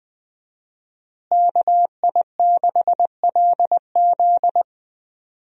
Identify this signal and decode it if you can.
Morse code, heard as a keyed tone, about 20 wpm, 700 Hz. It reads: KI6LZ
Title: USA Call Signs - 5 Characters - Repeated 3x - 20wpm